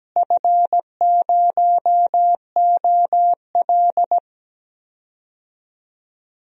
Morse code, heard as a keyed tone, about 17 wpm, 700 Hz. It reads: F0OL
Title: DXCC-100 Call Signs - 4 Characters - Repeated 3x - 17wpm